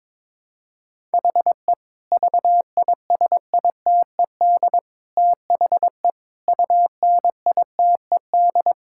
Morse code, heard as a keyed tone, about 22 wpm, 700 Hz. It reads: HE VISITED THE UNITED
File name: practice / 22wpm / Sentences with 4-Word Phrases - 22wpm